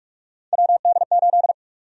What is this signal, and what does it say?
Morse code, heard as a keyed tone, about 46 wpm, 700 Hz. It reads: WD8
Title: USA Call Sign Prefixes - 45wpm